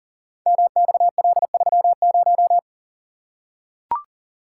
Morse code, heard as a keyed tone, about 42 wpm, 700 Hz. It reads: MXL30
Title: Code Groups - 5 Letter:Numbers - 40wpm